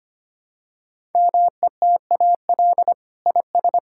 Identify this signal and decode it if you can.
Morse code, heard as a keyed tone, about 25 wpm, 700 Hz. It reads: METAL IS
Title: Sentences from Top 1000 Words - Review Part 6 - 25wpm